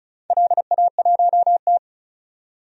Morse code, heard as a keyed tone, about 36 wpm, 700 Hz. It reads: LA1T
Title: DXCC-100 Call Signs - 4 Characters - 35wpm